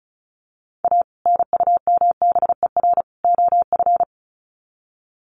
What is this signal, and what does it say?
Morse code, heard as a keyed tone, about 34 wpm, 700 Hz. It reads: A NUMBER OF
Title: Sets of 3 Words 35wpm